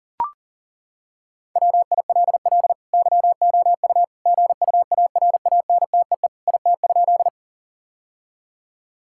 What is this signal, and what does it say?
Morse code, heard as a keyed tone, about 40 wpm, 700 Hz. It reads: WILL YOU GUARANTEE IT?
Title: Sentences from Top 1600 Words 40wpm